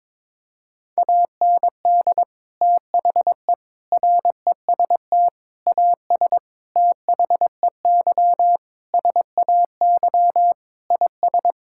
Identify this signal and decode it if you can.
Morse code, heard as a keyed tone, about 21 wpm, 700 Hz. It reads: AND THE REST AS THEY SAY IS
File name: practice / 22wpm / Sets of 8 Words - 22wpm